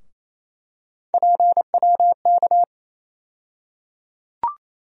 Morse code, heard as a keyed tone, about 28 wpm, 700 Hz. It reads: PWK
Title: USA Call Sign Postfixes - 3 Letters 28wpm